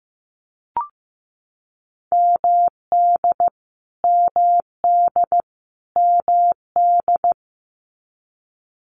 Morse code, heard as a keyed tone, about 15 wpm, 700 Hz. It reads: MD MD MD
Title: US States Repeated 3x 15wpm